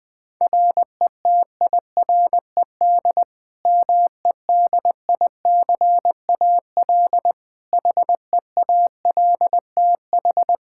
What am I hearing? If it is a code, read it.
Morse code, heard as a keyed tone, about 20 wpm, 700 Hz. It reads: RETIRED MEDICAL HEALTH
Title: K1USN SST QSO Practice 20wpm